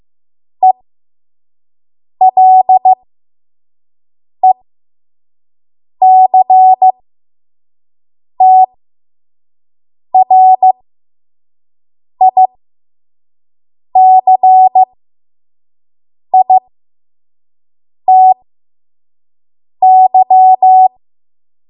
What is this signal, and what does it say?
Morse code, heard as a keyed tone, about 15 wpm, spaced as for 5 wpm, 750 Hz. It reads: ELECTRICITY
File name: Morse / 05WPM